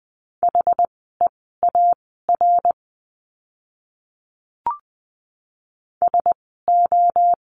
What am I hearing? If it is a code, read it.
Morse code, heard as a keyed tone, about 20 wpm, spaced as for 14 wpm, 700 Hz. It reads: HEAR SO
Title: 230 Lesson 37f - Letter x - Words 20wpm